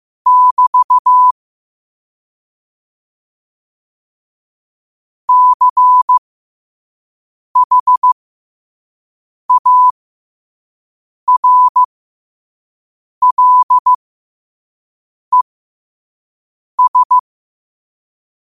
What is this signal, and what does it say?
Morse code, heard as a keyed tone, about 15 wpm, spaced as for 5 wpm, 1000 Hz. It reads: <BT> CHARLES
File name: qotdfiles / 5wpm